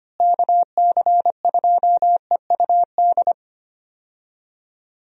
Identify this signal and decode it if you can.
Morse code, heard as a keyed tone, about 25 wpm, 700 Hz. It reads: KC2EUD
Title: USA Call Signs - 6 Characters - Repeated 2x - 25wpm